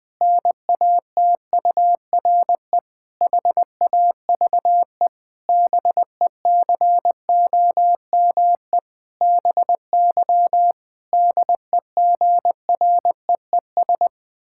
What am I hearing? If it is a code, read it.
Morse code, heard as a keyed tone, about 20 wpm, 700 Hz. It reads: NATURE HAVE BECOME BY DEGREES
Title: Pangram Challenge v2 - 20wpm